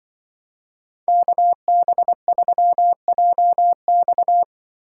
Morse code, heard as a keyed tone, about 24 wpm, 700 Hz. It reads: KB3JX
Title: Speed Racing - USA 2x2 Call Signs 20wpm